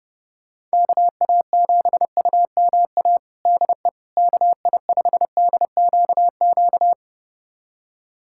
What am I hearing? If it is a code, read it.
Morse code, heard as a keyed tone, about 32 wpm, 700 Hz. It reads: KA7UMA DE KI5DQQ
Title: USA Callsign DE Callsign - 6 Characters - 30wpm